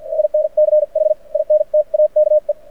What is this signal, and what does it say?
Morse code, heard as a keyed tone, about 53 wpm, spaced as for 43 wpm, 610 Hz. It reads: INCH INTAKE
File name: archived / CRSnet